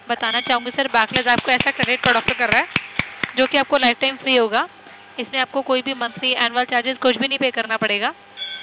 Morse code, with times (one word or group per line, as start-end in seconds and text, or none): none